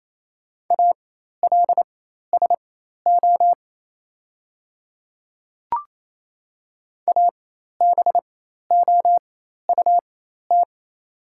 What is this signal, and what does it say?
Morse code, heard as a keyed tone, about 28 wpm, spaced as for 13 wpm, 700 Hz. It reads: ALSO ABOUT
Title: 4x Character Spacing - Top 200 Words - 28wpm